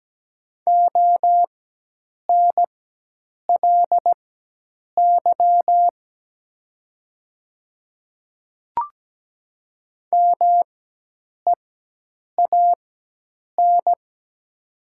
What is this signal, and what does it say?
Morse code, heard as a keyed tone, about 17 wpm, spaced as for 8 wpm, 700 Hz. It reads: ONLY MEAN
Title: 4x Character Spacing - Top 500 Words - 17wpm